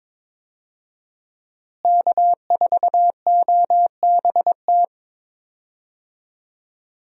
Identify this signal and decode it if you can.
Morse code, heard as a keyed tone, about 21 wpm, 700 Hz. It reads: K4OBT
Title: USA Call Signs - 5 Characters - Repeated 3x - 22wpm